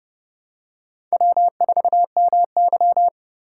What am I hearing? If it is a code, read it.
Morse code, heard as a keyed tone, about 30 wpm, 700 Hz. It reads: W4MY